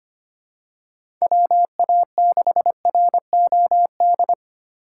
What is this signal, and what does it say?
Morse code, heard as a keyed tone, about 25 wpm, 700 Hz. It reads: WA6ROD
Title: USA Call Signs - 6 Characters - 3x Character Spacing - 25wpm